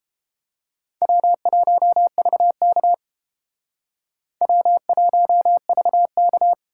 Morse code, heard as a keyed tone, about 31 wpm, 700 Hz. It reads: W1VK W1VK